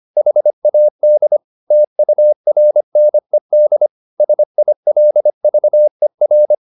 Morse code, heard as a keyed tone, about 25 wpm, 600 Hz. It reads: HAD TURNED SILVER